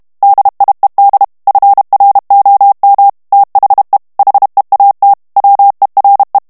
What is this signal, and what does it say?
Morse code, heard as a keyed tone, about 32 wpm, 800 Hz. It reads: DIED FROM THE HEAT WERE
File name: ANGLAIS texte clair 2 vitesse 30 mots minute